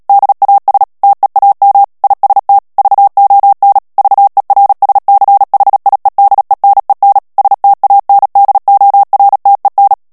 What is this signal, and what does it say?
Morse code, heard as a keyed tone, about 37 wpm, 800 Hz. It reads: DAS TEAM IST VON VERSCHIEDENEN STANDORTEN
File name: morse mp3 / ALLEMAND texte clair 2 vitesse 35 mots minute